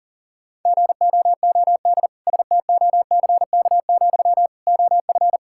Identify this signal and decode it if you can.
Morse code, heard as a keyed tone, about 40 wpm, 700 Hz. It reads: GOOD STOCK, OF
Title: Sentences from Top 1000 Words - Review Part 5 - 40wpm